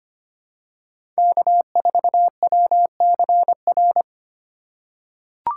Morse code, heard as a keyed tone, about 25 wpm, 700 Hz. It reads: K4WCR
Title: USA Call Signs - 3x Character Spacing - 25wpm